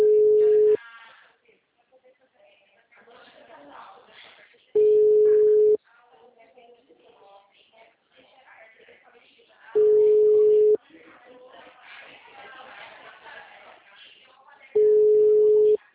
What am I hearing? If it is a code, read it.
Ringback tone (Europe, 425 Hz), with speech.
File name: out- -2014- - - .86764